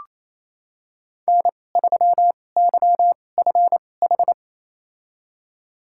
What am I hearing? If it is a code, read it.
Morse code, heard as a keyed tone, about 28 wpm, spaced as for 20 wpm, 700 Hz. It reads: N3YFH
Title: USA Call Signs - 5 Characters - 2x Character Spacing - 28wpm